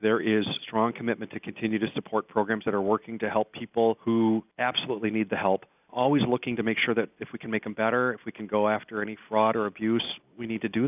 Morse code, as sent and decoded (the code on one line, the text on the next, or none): none